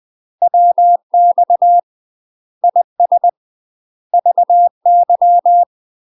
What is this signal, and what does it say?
Morse code, heard as a keyed tone, about 20 wpm, 700 Hz. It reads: WX IS VY
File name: UK MASTER QSOs TRAINING / 20WPM / UK MASTER QSOs 0010